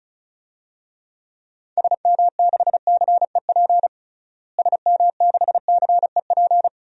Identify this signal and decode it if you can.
Morse code, heard as a keyed tone, about 34 wpm, 700 Hz. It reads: SM6CEP SM6CEP
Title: DXCC-100 Call Signs - 6 Characters - Repeated 3x - 35wpm